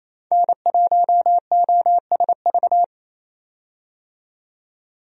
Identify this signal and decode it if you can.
Morse code, heard as a keyed tone, about 28 wpm, 700 Hz. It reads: N1OSV
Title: USA Call Signs - 5 Characters - 28wpm